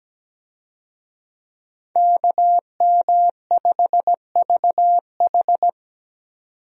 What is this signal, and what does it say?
Morse code, heard as a keyed tone, about 17 wpm, 700 Hz. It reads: KM5VH
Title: USA Call Signs - 5 Characters - 17wpm